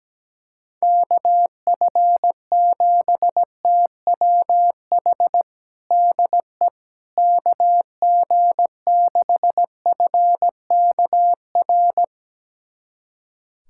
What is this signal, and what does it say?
Morse code, heard as a keyed tone, about 17 wpm, 700 Hz. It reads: KF7TWH DE KG6FKR
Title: USA Callsign DE Callsign - 6 Characters - 17wpm